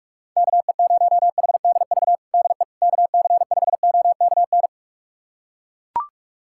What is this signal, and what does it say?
Morse code, heard as a keyed tone, about 41 wpm, 700 Hz. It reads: KE0HDV DE KC5OKN
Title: USA Callsign DE Callsign 45wpm